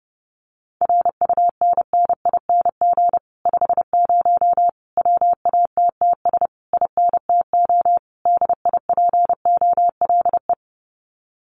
Morse code, heard as a keyed tone, about 32 wpm, 700 Hz. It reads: RUNNING 50 WATTS INTO DIPOLE
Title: Common QSO Phrases Encore 30wpm